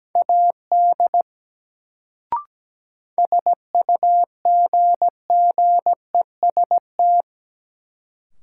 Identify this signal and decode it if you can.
Morse code, heard as a keyed tone, about 17 wpm, 700 Hz. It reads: AD SUGGEST
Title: Sentences from Top 1300 Words - New Word Drill - 17wpm